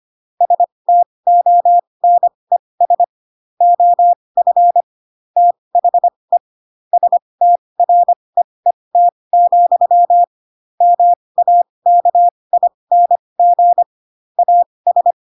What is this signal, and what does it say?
Morse code, heard as a keyed tone, about 25 wpm, spaced as for 20 wpm, 700 Hz. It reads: STONES OF THE STREET, MAKING AS